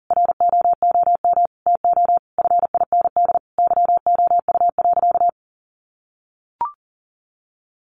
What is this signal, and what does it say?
Morse code, heard as a keyed tone, about 37 wpm, 700 Hz. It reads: ROOM TO FIND YOU.
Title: Sentences from Top 1300 Words 40wpm